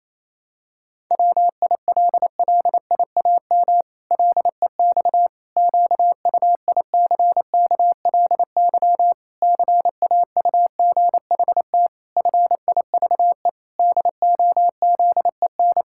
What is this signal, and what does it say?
Morse code, heard as a keyed tone, about 28 wpm, 700 Hz. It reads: WILLIAM LEX QUICKLY CAUGHT FIVE DOZEN
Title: Pangram Challenge v2 - 28wpm